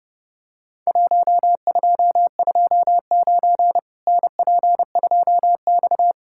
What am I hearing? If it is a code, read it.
Morse code, heard as a keyed tone, about 30 wpm, 700 Hz. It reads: 1229 NP2X
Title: Worked All Europe DX Contest - QTC Exchange - 30wpm